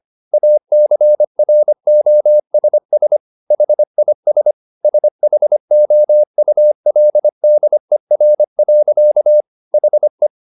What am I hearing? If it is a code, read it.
Morse code, heard as a keyed tone, about 25 wpm, 600 Hz. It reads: ACROSS HIS SHOULDER. HE